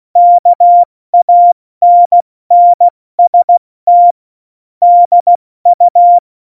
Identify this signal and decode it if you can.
Morse code, heard as a keyed tone, about 16 wpm, 700 Hz. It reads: KANNST DU
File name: Morsekurs / Witze / Witz 18. September 2018, 70 F80